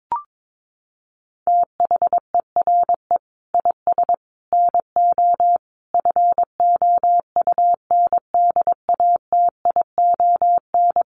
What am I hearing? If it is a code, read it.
Morse code, heard as a keyed tone, about 22 wpm, 700 Hz. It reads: THERE IS NO FOUNDATION
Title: Sentences from Top 1200 Words 22wpm